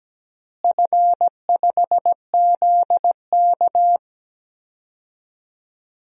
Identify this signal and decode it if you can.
Morse code, heard as a keyed tone, about 17 wpm, 700 Hz. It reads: F5ZK
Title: DXCC-100 Call Signs - 4 Characters - Repeated 2x - 17wpm